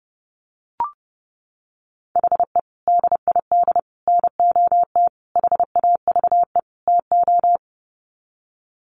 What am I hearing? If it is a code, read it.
Morse code, heard as a keyed tone, about 30 wpm, 700 Hz. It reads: HE DID NOT HAVE TO
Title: Sets of 5 Words 30wpm